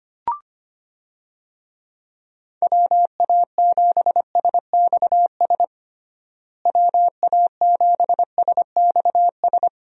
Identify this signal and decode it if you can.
Morse code, heard as a keyed tone, about 25 wpm, 700 Hz. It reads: WA7SXS WA7SXS
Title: USA Call Signs - 6 Characters - Repeated 3x - 25wpm